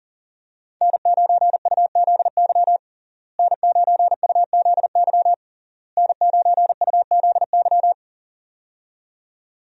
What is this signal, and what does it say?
Morse code, heard as a keyed tone, about 40 wpm, 700 Hz. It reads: N9UZY N9UZY N9UZY
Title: USA Call Signs - 5 Characters - Repeated 3x - 40wpm